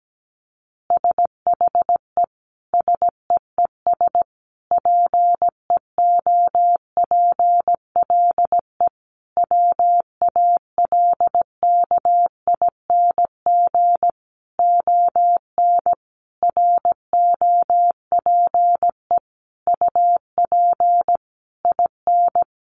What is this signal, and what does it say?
Morse code, heard as a keyed tone, about 17 wpm, 700 Hz. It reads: SHE SEES PEOPLE WALKING ON ROPE UP IN
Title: Easy Phrases 17wpm